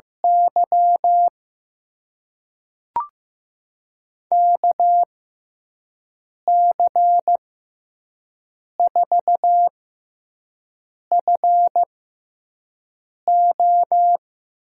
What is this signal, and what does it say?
Morse code, heard as a keyed tone, about 15 wpm, spaced as for 5 wpm, 700 Hz. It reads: Y KC4FO
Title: USA Call Signs - 6x Character Spacing - 15wpm